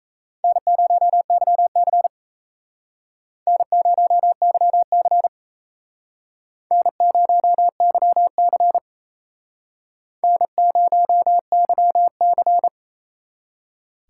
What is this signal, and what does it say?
Morse code, heard as a keyed tone, about 36 wpm, 700 Hz. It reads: N0YC N0YC N0YC N0YC